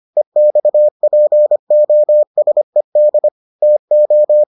Morse code, heard as a keyed tone, about 25 wpm, 600 Hz. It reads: EXPOSED TO